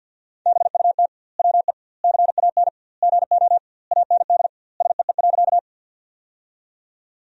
Morse code, heard as a keyed tone, about 52 wpm, 700 Hz. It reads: BUT WE CAN GO AND SEE.
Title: 2x Word-Spacing - Sentences from Top 100 Words - 50wpm